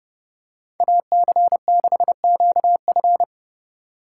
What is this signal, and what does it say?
Morse code, heard as a keyed tone, about 30 wpm, 700 Hz. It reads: AC6QF